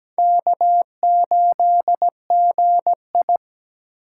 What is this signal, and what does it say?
Morse code, heard as a keyed tone, about 17 wpm, 700 Hz. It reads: K8GI
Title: USA Call Signs - 4 Characters - 4x Character Spacing - 17wpm